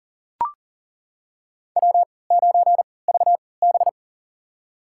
Morse code, heard as a keyed tone, about 40 wpm, spaced as for 23 wpm, 700 Hz. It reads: W9VB
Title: USA Call Signs - 4 Characters - 3x Character Spacing - 40wpm